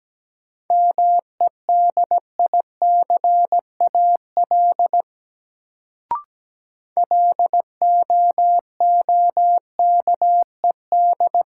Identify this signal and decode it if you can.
Morse code, heard as a keyed tone, about 17 wpm, 700 Hz. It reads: MEDICAL LOOKED
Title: Sentences from Top 800 Words - New Word Drill - 17wpm